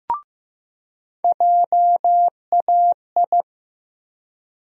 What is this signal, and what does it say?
Morse code, heard as a keyed tone, about 15 wpm, 700 Hz. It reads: JAI
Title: USA Call Sign Postfixes - 3 Letters 15wpm